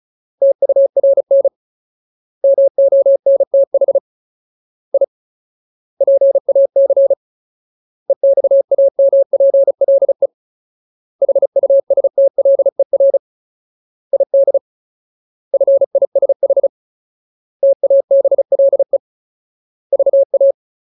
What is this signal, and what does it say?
Morse code, heard as a keyed tone, about 34 wpm, 550 Hz. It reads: TURN MONTH I PAC EXAMPLE HUSTLER ID FISH TABLE VA
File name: Ultimate Word List / 35WPM Ultimate